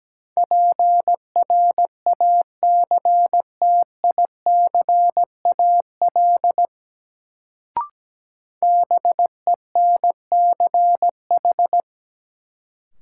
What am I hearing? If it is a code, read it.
Morse code, heard as a keyed tone, about 17 wpm, 700 Hz. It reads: PRACTICAL BENCH
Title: Sentences from Top 1400 Words - New Word Drill Alt - 17wpm